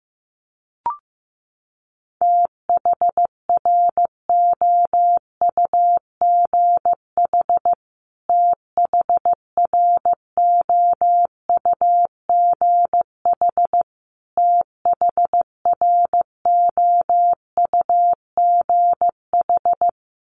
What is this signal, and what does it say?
Morse code, heard as a keyed tone, about 15 wpm, 700 Hz. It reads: THROUGH THROUGH THROUGH
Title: Top 200 Words Repeated 15wpm